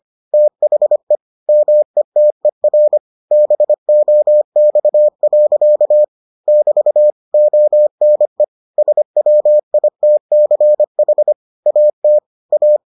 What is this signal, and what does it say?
Morse code, heard as a keyed tone, about 25 wpm, 600 Hz. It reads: THE METER BOX. <BT> ONE SWITCH AT A